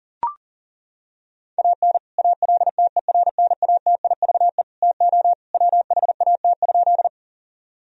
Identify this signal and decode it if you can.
Morse code, heard as a keyed tone, about 37 wpm, 700 Hz. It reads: AN ALTERNATIVE TO WHAT?
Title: Sentences from Top 1000 Words - Review Part 4 - 40wpm